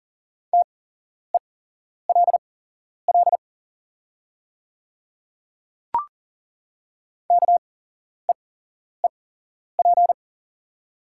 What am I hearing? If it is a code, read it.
Morse code, heard as a keyed tone, about 41 wpm, spaced as for 11 wpm, 700 Hz. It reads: TELL KEEP
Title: 8x Character Spacing - Top 500 Words - 40wpm